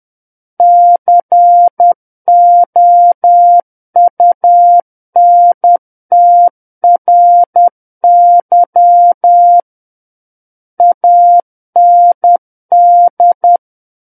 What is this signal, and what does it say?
Morse code, heard as a keyed tone, about 10 wpm, 700 Hz. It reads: COUNTRY AND
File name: Book One / 01-Chapter 1 - 10 WPM